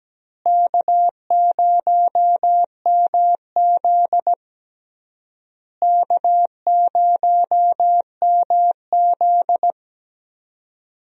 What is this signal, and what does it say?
Morse code, heard as a keyed tone, about 17 wpm, 700 Hz. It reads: K0MZ K0MZ
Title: USA Call Signs - 4 Characters - Repeated 3x - 17wpm